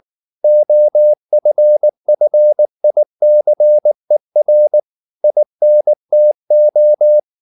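Morse code, heard as a keyed tone, about 19 wpm, 600 Hz. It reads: OFFICER INTO
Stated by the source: Morse code at 19 wpm from CW med Gnister, reading Ken Follett's novel Never in English.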